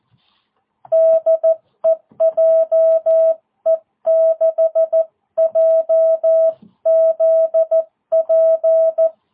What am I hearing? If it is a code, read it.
Morse code, heard as a keyed tone, about 14 wpm, 660 Hz. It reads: DE JE6JZP